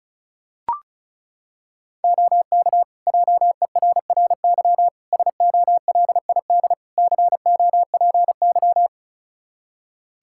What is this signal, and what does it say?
Morse code, heard as a keyed tone, about 36 wpm, 700 Hz. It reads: OK JERRY SOLID COPY
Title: Practice QSO - Exchange 3 - More Info - 35wpm